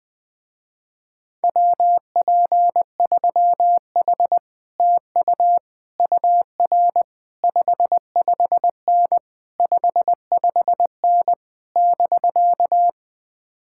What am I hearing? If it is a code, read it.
Morse code, heard as a keyed tone, about 20 wpm, 700 Hz. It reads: WP3H TU UR 55N 55N <BK>
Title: Speed-Racing Common QSO Phrases - 17wpm